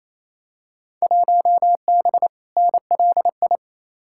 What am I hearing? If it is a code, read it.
Morse code, heard as a keyed tone, about 28 wpm, 700 Hz. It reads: 1B NLI
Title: Field Day Exchanges Encore - 28wpm